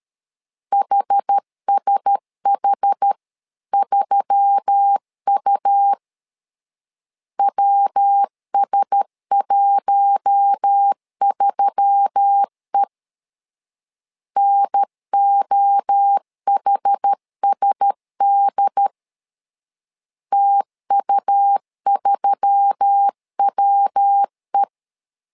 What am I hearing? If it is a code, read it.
Morse code, heard as a keyed tone, about 13 wpm, 800 Hz. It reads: HSH3U WS13E NOHSD TU3WE
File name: LEZIONI / 4 Q1S3 12wpm 2023